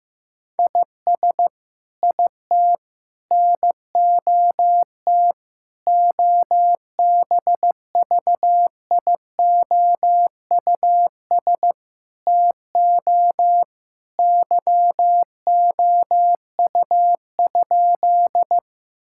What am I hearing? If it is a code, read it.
Morse code, heard as a keyed tone, about 15 wpm, 700 Hz. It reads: IS IT NOT OBVIOUS TO YOU?
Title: Sentences from Top 1600 Words 15wpm